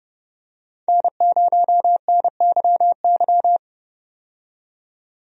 Morse code, heard as a keyed tone, about 30 wpm, 700 Hz. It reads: N0NYY
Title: USA Call Signs - 5 Characters - 30wpm